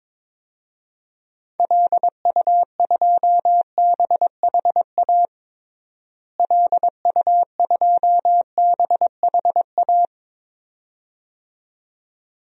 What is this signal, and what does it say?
Morse code, heard as a keyed tone, about 22 wpm, 700 Hz. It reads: LU2BHA LU2BHA
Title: DXCC-100 Call Signs - 6 Characters - Repeated 2x - 22wpm